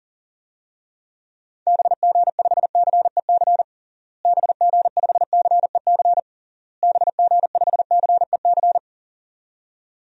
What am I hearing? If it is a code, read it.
Morse code, heard as a keyed tone, about 40 wpm, 700 Hz. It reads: BG5CEC BG5CEC BG5CEC
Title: DXCC-100 Call Signs - 6 Characters - Repeated 3x - 40wpm